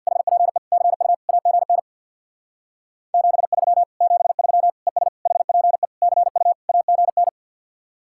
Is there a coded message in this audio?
Yes: Morse code from 0.1 to 7.3 s, heard as a keyed tone, about 65 wpm, 700 Hz. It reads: HPE CU AGN 73 73 ES HPE CU AGN